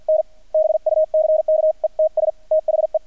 Morse code, heard as a keyed tone, about 50 wpm, spaced as for 44 wpm, 630 Hz, with radio noise. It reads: N BUCKETS THE
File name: archived / CRSnet